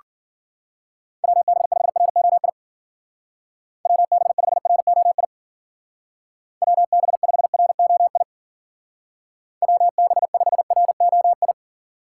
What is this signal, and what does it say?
Morse code, heard as a keyed tone, about 49 wpm, 700 Hz. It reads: WB5ROI WB5ROI WB5ROI WB5ROI
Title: Speed Racing - USA 2x3 Call Signs 40wpm